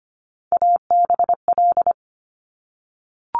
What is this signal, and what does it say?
Morse code, heard as a keyed tone, about 25 wpm, 700 Hz. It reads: ABL
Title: Top Three Letter Combinations in Words 25wpm